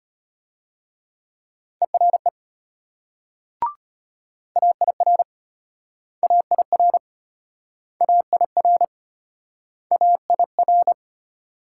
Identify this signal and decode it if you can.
Morse code, heard as a keyed tone, about 33 wpm, 700 Hz. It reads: ERE AIR AIR AIR AIR